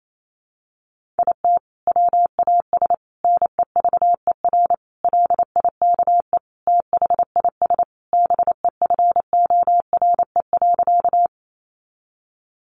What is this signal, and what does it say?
Morse code, heard as a keyed tone, about 28 wpm, 700 Hz. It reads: IT WAS NEVER LIKE THIS BEFORE.
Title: A Sound Of Thunder 28wpm